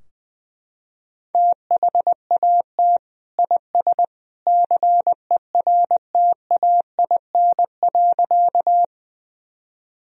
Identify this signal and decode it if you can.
Morse code, heard as a keyed tone, about 21 wpm, 700 Hz. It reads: THAT IS CERTAIN.
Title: Sentences from Top 1000 Words - Review Part 2 - 20wpm